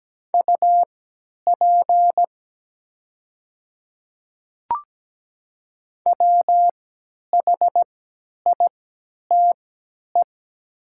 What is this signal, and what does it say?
Morse code, heard as a keyed tone, about 17 wpm, spaced as for 10 wpm, 700 Hz. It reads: UP WHITE
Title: 3x Character Spacing - Top 300 Words - 17wpm